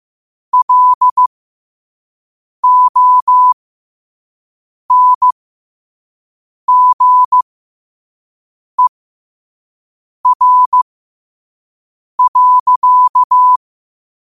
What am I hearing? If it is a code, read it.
Morse code, heard as a keyed tone, about 15 wpm, spaced as for 5 wpm, 1000 Hz. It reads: LONGER.
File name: qotdfiles / 5wpm